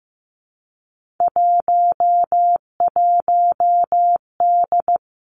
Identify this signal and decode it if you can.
Morse code, heard as a keyed tone, about 15 wpm, 700 Hz. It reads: 11D
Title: Three Challenging Letter-Number Combinations 15wpm